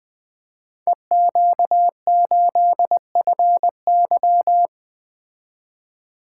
Morse code, heard as a keyed tone, about 20 wpm, 700 Hz. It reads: EQ8FY
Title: Code Groups - 5 Letter:Numbers - 20wpm